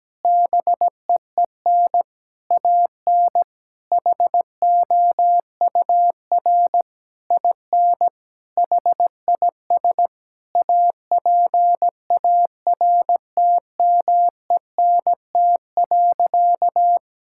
Morse code, heard as a keyed tone, about 17 wpm, 700 Hz. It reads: BEEN AN HOUR IN HIS APARTMENT.
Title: Sentences from Top 1000 Words - Review Part 6 - 17wpm